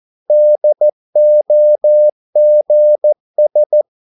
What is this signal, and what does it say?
Morse code, heard as a keyed tone, about 14 wpm, 600 Hz. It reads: DOGS
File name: mp3 files / a dogs mistake - 14wpm